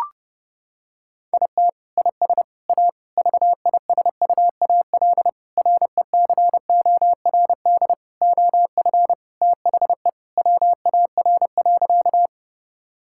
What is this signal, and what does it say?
Morse code, heard as a keyed tone, about 32 wpm, 700 Hz. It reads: IT IS A VISUAL RECORD OF THE WAR.
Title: Sentences from Top 1000 Words - Review Part 6 - 30wpm